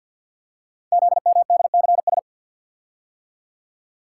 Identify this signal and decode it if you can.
Morse code, heard as a keyed tone, about 48 wpm, 700 Hz. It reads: ZMDCS
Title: Code Groups - 5 Letters - 50wpm